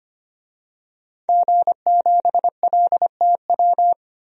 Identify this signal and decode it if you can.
Morse code, heard as a keyed tone, about 25 wpm, 700 Hz. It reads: G7LTW
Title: DXCC-100 Call Signs - 5 Characters - 25wpm